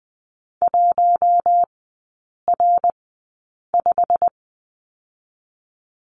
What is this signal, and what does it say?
Morse code, heard as a keyed tone, about 20 wpm, spaced as for 8 wpm, 700 Hz. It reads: 1R5
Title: 198 Lesson 32e - Letter k - Three Characters 20wpm